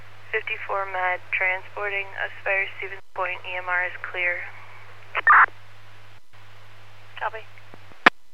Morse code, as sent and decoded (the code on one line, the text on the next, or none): none